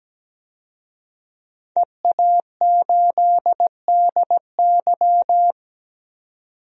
Morse code, heard as a keyed tone, about 17 wpm, 700 Hz. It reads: EA8DY
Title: DXCC-100 Call Signs - 5 Characters - 17wpm